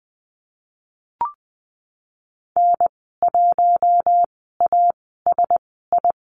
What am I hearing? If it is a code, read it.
Morse code, heard as a keyed tone, about 20 wpm, spaced as for 15 wpm, 700 Hz. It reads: N1ASI
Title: 076 Lesson 13g - Letter L - Call Signs 20wpm